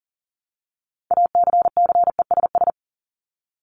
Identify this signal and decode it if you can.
Morse code, heard as a keyed tone, about 40 wpm, 700 Hz. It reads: ACCESS
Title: Top 501-750 Words 40wpm